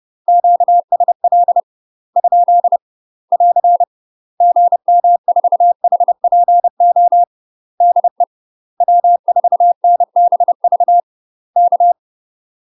Morse code, heard as a keyed tone, about 30 wpm, 700 Hz. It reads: QSL ? <AR> GM4HPO DE W4NBV K